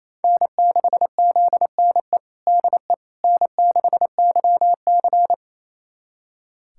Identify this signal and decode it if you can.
Morse code, heard as a keyed tone, about 28 wpm, 700 Hz. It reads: N6ZNE DE N6YC
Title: USA Callsign DE Callsign 28wpm